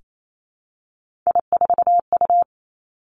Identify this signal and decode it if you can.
Morse code, heard as a keyed tone, about 28 wpm, 700 Hz. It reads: I4U
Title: Three Challenging Letter-Number Combinations 28wpm